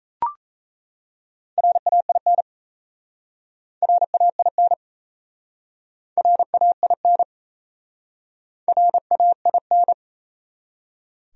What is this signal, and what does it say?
Morse code, heard as a keyed tone, about 38 wpm, 700 Hz. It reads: RAIN RAIN RAIN RAIN